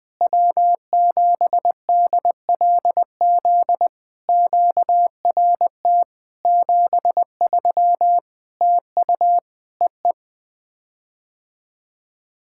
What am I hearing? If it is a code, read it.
Morse code, heard as a keyed tone, about 20 wpm, 700 Hz. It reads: W7DLZ QRT 73 TU EE
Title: POTA QSO Practice - 20wpm